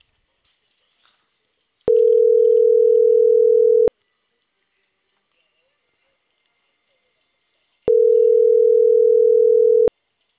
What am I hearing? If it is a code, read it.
Ringback tone (North America), 440 + 480 Hz.